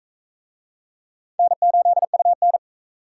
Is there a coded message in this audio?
Yes: Morse code from 1.4 to 2.6 s, heard as a keyed tone, about 38 wpm, 700 Hz. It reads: N8UN